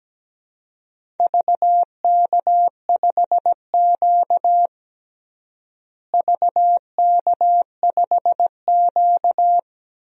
Morse code, heard as a keyed tone, about 17 wpm, 700 Hz. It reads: VK5Q VK5Q
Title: DXCC-100 Call Signs - 4 Characters - Repeated 2x - 17wpm